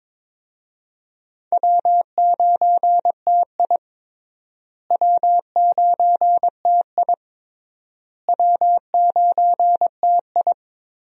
Morse code, heard as a keyed tone, about 21 wpm, 700 Hz. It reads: W9TI W9TI W9TI
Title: USA Call Signs - 4 Characters - Repeated 3x - 22wpm